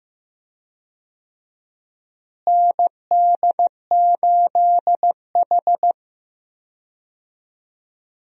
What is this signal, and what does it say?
Morse code, heard as a keyed tone, about 15 wpm, 700 Hz. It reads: ND8H
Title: USA Call Signs - 4 Characters - Repeated 3x - 15wpm